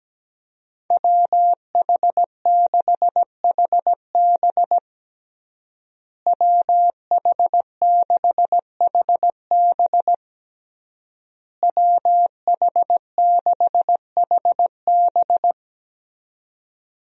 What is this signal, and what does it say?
Morse code, heard as a keyed tone, about 17 wpm, 700 Hz. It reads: WH6HB WH6HB WH6HB
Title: USA Call Signs - 5 Characters - Repeated 3x - 17wpm